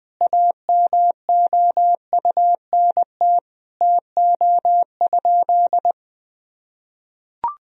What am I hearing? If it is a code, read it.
Morse code, heard as a keyed tone, about 21 wpm, 700 Hz. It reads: AMOUNT TO?
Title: Sentences from Top 1000 Words - Review Part 4 - 20wpm